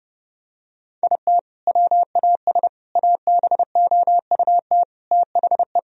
Morse code, heard as a keyed tone, about 30 wpm, 700 Hz. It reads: IT WAS ABOUT THE
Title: Sentences from Top 1000 Words - Review Part 1 - 30wpm